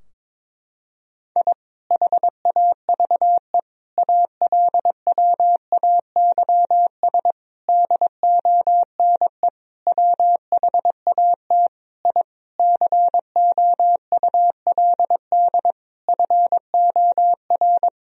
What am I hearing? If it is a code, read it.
Morse code, heard as a keyed tone, about 22 wpm, 700 Hz. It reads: I HAVE ALWAYS DONE WHAT I COULD FOR
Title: Sentences from Top 1000 Words - Review Part 2 - 22wpm